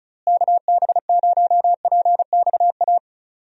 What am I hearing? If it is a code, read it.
Morse code, heard as a keyed tone, about 34 wpm, 700 Hz. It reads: KB0PXA
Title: USA Call Signs - 6 Characters - Repeated 3x - 35wpm